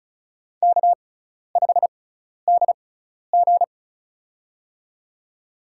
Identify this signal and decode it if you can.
Morse code, heard as a keyed tone, about 36 wpm, spaced as for 12 wpm, 700 Hz. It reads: K5DG
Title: USA Call Signs - 4 Characters - 6x Character Spacing - 35wpm